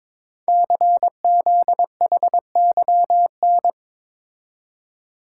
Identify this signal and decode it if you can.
Morse code, heard as a keyed tone, about 22 wpm, 700 Hz. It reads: CZHYN
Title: Code Groups - 5 Letters - 22wpm